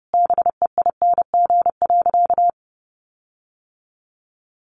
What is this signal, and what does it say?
Morse code, heard as a keyed tone, about 30 wpm, 700 Hz. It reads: BEING.
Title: Sentences from Top 1500 Words 30wpm